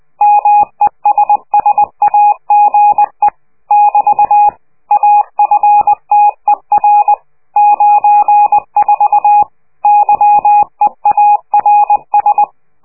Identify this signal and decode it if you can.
Morse code, heard as a keyed tone, about 20 wpm, 830 Hz. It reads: MESSAGE <BT> AFTER 94 YEARS
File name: worldcoastal / scheveningenradio / morsepch